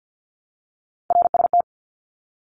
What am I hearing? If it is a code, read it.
Morse code, heard as a keyed tone, about 50 wpm, 700 Hz. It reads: RST
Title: Abbreviations 50wpm